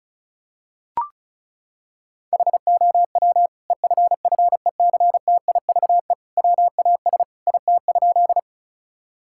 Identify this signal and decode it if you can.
Morse code, heard as a keyed tone, about 36 wpm, 700 Hz. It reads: HOW EFFECTIVE WAS IT?
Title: Sentences from Top 1000 Words - Review Part 6 - 35wpm